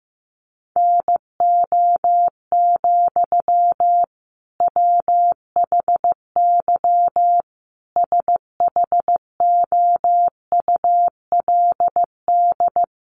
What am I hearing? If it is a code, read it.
Morse code, heard as a keyed tone, about 15 wpm, 700 Hz. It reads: NO, WHY SHOULD
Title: Sentences from Top 300 Words 15wpm